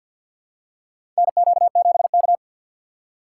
Morse code, heard as a keyed tone, about 52 wpm, 700 Hz. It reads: NQ7K
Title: USA Call Signs - 6x Character Spacing - 50wpm